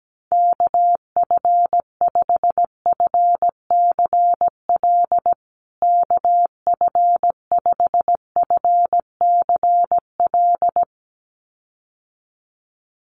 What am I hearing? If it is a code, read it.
Morse code, heard as a keyed tone, about 17 wpm, 700 Hz. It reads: KF5FCL KF5FCL
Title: USA Call Signs Repeated 2x 17wpm